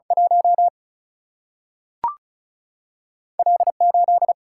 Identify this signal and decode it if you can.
Morse code, heard as a keyed tone, about 36 wpm, 700 Hz. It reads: 1 L 8
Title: Single-Letter Single-Number 35wpm